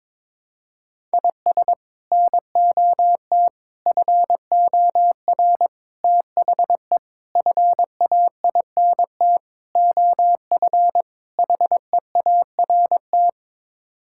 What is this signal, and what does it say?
Morse code, heard as a keyed tone, about 22 wpm, 700 Hz. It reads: IS NOT FOR THE FAINT OF HEART
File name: practice / 22wpm / Sets of 7 Words - 22wpm